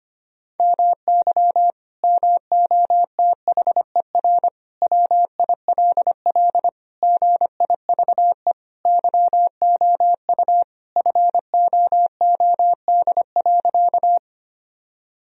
Morse code, heard as a keyed tone, about 25 wpm, 700 Hz. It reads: MY MOTHER WILL GIVE YOU FOOD.
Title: Sentences from Top 1000 Words - Review Part 1 - 25wpm